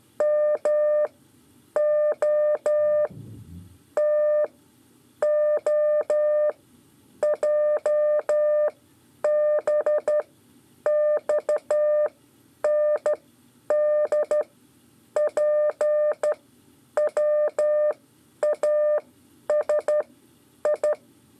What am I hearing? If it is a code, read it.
Morse code, heard as a keyed tone, about 12 wpm, spaced as for 8 wpm, 600 Hz. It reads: MO TOJBXNDPWASI
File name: 5 LEZIONE RIPASSO 1